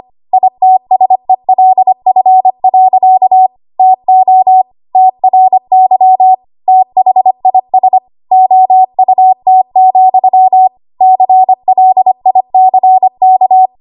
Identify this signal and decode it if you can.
Morse code, heard as a keyed tone, about 25 wpm, 750 Hz. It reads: ITSELF. TO TRY THIS OUT, CLICK
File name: Archive / 25 WPM / 25WPM